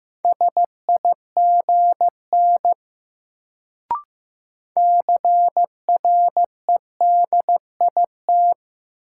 Morse code, heard as a keyed tone, about 15 wpm, 700 Hz. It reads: SIGN CREDIT
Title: Sentences from Top 1000 Words - New Word Drill - 15wpm